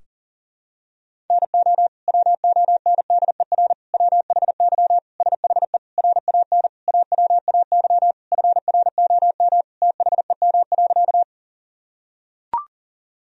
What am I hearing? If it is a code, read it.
Morse code, heard as a keyed tone, about 40 wpm, 700 Hz. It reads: NO WONDER WHY SHE RAN AWAY FROM THEM.
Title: Sentences from Top 1000 Words - Review Part 3 - 40wpm